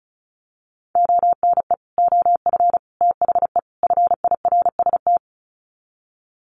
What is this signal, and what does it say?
Morse code, heard as a keyed tone, about 34 wpm, 700 Hz. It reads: ONE OF THE FIRST
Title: Sets of 4 Words 35wpm